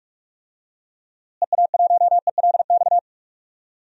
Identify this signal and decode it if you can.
Morse code, heard as a keyed tone, about 41 wpm, 700 Hz. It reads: EA1ELX
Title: DXCC-100 Call Signs - 6 Characters - Repeated 2x - 45wpm